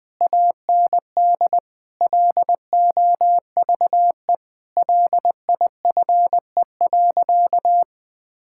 Morse code, heard as a keyed tone, about 19 wpm, 700 Hz. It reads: AND LOVE LIFE.
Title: Sentences with 8-Word Phrases - Spoken Beginning - 20wpm